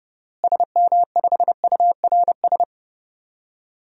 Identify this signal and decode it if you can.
Morse code, heard as a keyed tone, about 30 wpm, 700 Hz. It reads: SM5URS
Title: DXCC-100 Call Signs - 6 Characters - Repeated 3x - 30wpm